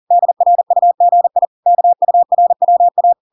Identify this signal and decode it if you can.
Morse code, heard as a keyed tone, about 40 wpm, 700 Hz. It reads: DRUGI KURWA